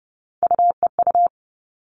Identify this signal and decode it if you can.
Morse code, heard as a keyed tone, about 30 wpm, 700 Hz. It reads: UEU
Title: All Cut Numbers - 3-Digit 30wpm